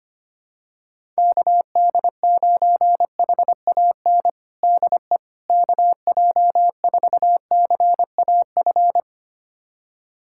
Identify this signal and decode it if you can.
Morse code, heard as a keyed tone, about 25 wpm, 700 Hz. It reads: KD9HAN DE KJ4CAF
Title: USA Callsign DE Callsign 25wpm